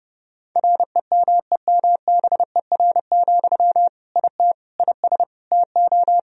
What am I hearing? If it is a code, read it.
Morse code, heard as a keyed tone, about 30 wpm, 700 Hz. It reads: REMEMBER, IT IS TO
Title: Sentences from Top 1000 Words - Review Part 2 - 30wpm